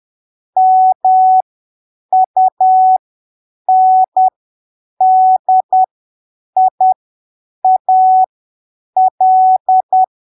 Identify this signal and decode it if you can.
Morse code, heard as a keyed tone, about 10 wpm, spaced as for 7 wpm, 750 Hz. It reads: MUNDIAL
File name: arquivos / texto / texto 14 10ppm